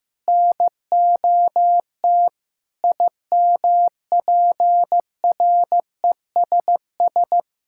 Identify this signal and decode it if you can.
Morse code, heard as a keyed tone, about 15 wpm, 700 Hz. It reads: NOT IMPRESS
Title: Sentences from Top 1600 Words 15wpm